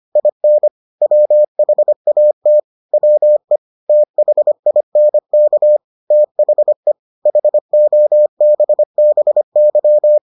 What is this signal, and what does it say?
Morse code, heard as a keyed tone, about 25 wpm, 600 Hz. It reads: IN WHAT WE THINK THE HOBBY